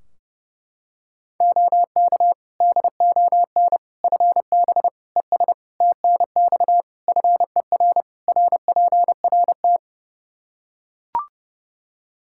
Morse code, heard as a keyed tone, about 30 wpm, 700 Hz. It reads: OK DON FB ES TNX FER RPRT
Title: Practice QSO - Exchange 2 - Info - 30wpm